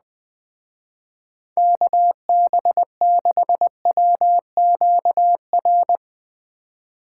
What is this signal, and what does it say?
Morse code, heard as a keyed tone, about 21 wpm, 700 Hz. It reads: KB6WQR